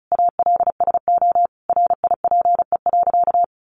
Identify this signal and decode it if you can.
Morse code, heard as a keyed tone, about 34 wpm, 700 Hz. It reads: ALSO RIPE.
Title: Additional Easy Phrases 35wpm